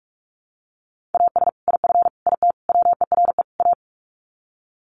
Morse code, heard as a keyed tone, about 44 wpm, 700 Hz. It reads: AS IF IT WERE A
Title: Sets of 5 Words 45wpm